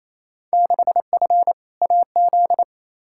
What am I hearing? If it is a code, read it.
Morse code, heard as a keyed tone, about 28 wpm, 700 Hz. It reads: 6F AZ
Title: Field Day Exchanges Encore - 28wpm